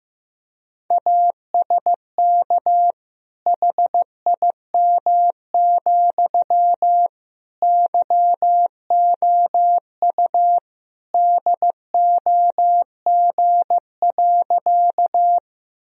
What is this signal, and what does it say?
Morse code, heard as a keyed tone, about 15 wpm, 700 Hz. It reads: ASK HIM, YOU DOG.
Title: Sentences from Top 1000 Words - Review Part 2 - 15wpm